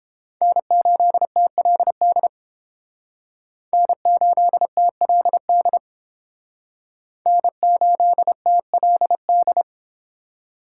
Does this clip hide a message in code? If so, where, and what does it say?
Yes: Morse code from 0.4 to 9.6 s, heard as a keyed tone, about 30 wpm, 700 Hz. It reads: N8TLD N8TLD N8TLD